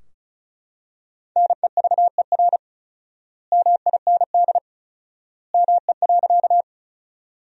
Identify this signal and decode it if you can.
Morse code, heard as a keyed tone, about 34 wpm, 700 Hz. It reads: NEVER MIND ME.
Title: 4x Word-Spacing - Sentences from Top 500 Words - 35wpm